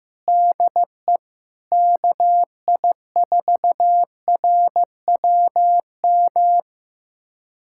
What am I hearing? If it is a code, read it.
Morse code, heard as a keyed tone, about 16 wpm, 700 Hz. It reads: DE KI4RWM
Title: USA Callsign DE Callsign - 6 Characters - 15wpm